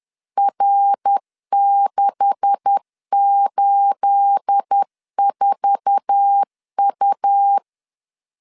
Morse code, heard as a keyed tone, about 10 wpm, 800 Hz. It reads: R684U
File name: LEZIONI / 10 SIGNS / 10A SLASH 10wpm 2808